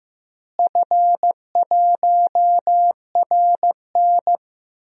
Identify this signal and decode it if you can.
Morse code, heard as a keyed tone, about 15 wpm, 700 Hz. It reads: F1RN
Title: DXCC-100 Call Signs - 4 Characters - Repeated 2x - 15wpm